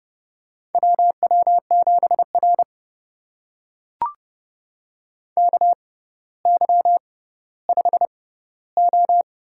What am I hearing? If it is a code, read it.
Morse code, heard as a keyed tone, about 30 wpm, spaced as for 15 wpm, 700 Hz. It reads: WW7R KY5O
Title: USA Call Signs - 4 Characters - 6x Character Spacing - 30wpm